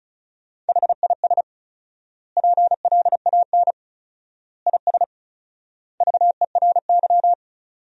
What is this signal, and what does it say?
Morse code, heard as a keyed tone, about 36 wpm, 700 Hz. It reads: HIS PLAN IS VERY
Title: 4x Word-Spacing - Sentences from Top 500 Words - 35wpm